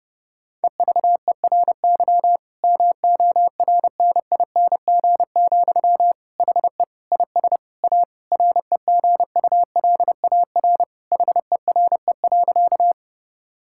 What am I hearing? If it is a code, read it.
Morse code, heard as a keyed tone, about 30 wpm, 700 Hz. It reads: EVERY MORNING, HE IS A REGULAR HERE.
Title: Sentences from Top 1000 Words - Review Part 6 - 30wpm